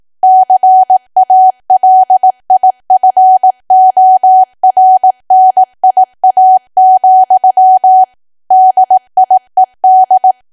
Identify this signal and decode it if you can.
Morse code, heard as a keyed tone, about 18 wpm, 750 Hz. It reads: CALIFORNIA, DIED
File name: Archive / 18 WPM / 18WPM